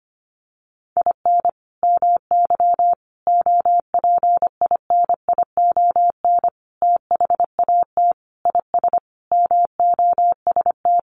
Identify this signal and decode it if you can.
Morse code, heard as a keyed tone, about 25 wpm, 700 Hz. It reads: IN MY OPINION THAT IS MOST
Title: Sentences from Top 1200 Words 25wpm